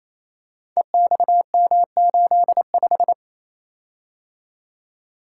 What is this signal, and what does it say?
Morse code, heard as a keyed tone, about 27 wpm, 700 Hz. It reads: EXM85
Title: Code Groups - 5 Letter:Numbers - 28wpm